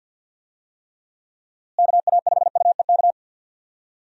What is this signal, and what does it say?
Morse code, heard as a keyed tone, about 52 wpm, 700 Hz. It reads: KA5UEK
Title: USA Call Signs - 6 Characters - Repeated 3x - 50wpm